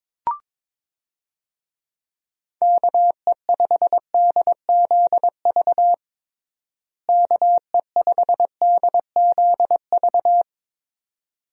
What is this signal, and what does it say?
Morse code, heard as a keyed tone, about 22 wpm, 700 Hz. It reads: KE5DZV KE5DZV
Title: USA Call Signs - 6 Characters - Repeated 2x - 22wpm